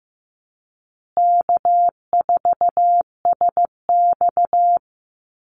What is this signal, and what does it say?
Morse code, heard as a keyed tone, about 15 wpm, 700 Hz. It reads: K4SX
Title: Easy Call Signs Repeated 3x - 15wpm